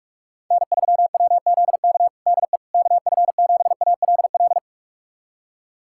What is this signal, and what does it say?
Morse code, heard as a keyed tone, about 41 wpm, 700 Hz. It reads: N3WZK DE KF7ALL
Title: USA Callsign DE Callsign 45wpm